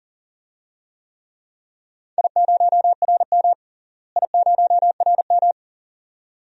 Morse code, heard as a keyed tone, about 40 wpm, 690 Hz. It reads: I0RM I0RM
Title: DXCC-100 Call Signs - 4 Characters - Repeated 2x - 40wpm